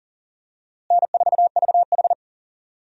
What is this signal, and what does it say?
Morse code, heard as a keyed tone, about 40 wpm, 700 Hz. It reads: N4VH
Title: USA Call Signs - 4 Characters - Repeated 3x - 40wpm